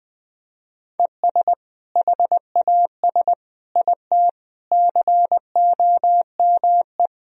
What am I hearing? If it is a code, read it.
Morse code, heard as a keyed tone, about 20 wpm, 700 Hz. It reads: ES HAS IT COME
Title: Sentences from Top 100 Words and Abbreviations 20wpm